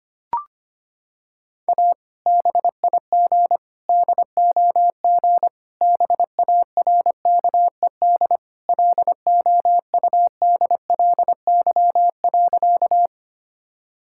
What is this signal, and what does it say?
Morse code, heard as a keyed tone, about 25 wpm, 700 Hz. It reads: A BIG DOG BARKED LOUDLY.
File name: practice / 25wpm / Sentences with 2-Word Phrases - 25wpm